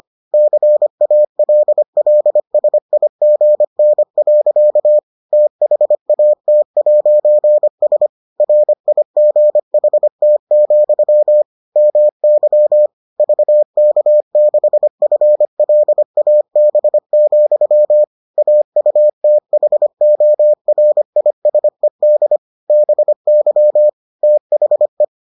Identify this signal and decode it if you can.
Morse code, heard as a keyed tone, about 25 wpm, 600 Hz. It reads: CALLSIGN. THAT'S RIGHT, MY VK6FLAB, AUTHORISED BY THE